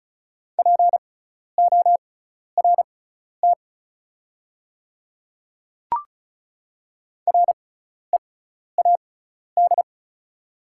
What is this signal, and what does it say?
Morse code, heard as a keyed tone, about 36 wpm, spaced as for 12 wpm, 690 Hz. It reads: PORT READ
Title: 6x Character Spacing - Top 500 Words - 35wpm